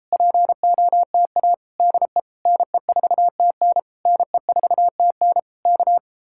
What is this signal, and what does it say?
Morse code, heard as a keyed tone, about 38 wpm, 700 Hz. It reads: POTA DE NE4TN NE4TN K